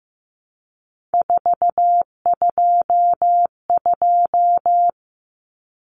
Morse code, heard as a keyed tone, about 15 wpm, 700 Hz. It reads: 422
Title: Numbers - 3-Digit 15wpm